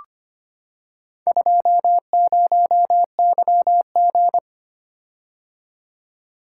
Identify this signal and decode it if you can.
Morse code, heard as a keyed tone, about 25 wpm, 700 Hz. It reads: 20YG
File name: practice / 25wpm / Four Letter-Number Combinations - 25wpm